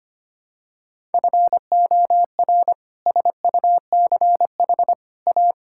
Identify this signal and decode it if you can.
Morse code, heard as a keyed tone, about 25 wpm, 700 Hz. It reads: FOR SUCH A
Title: Sentences from Top 1000 Words - Review Part 6 - 25wpm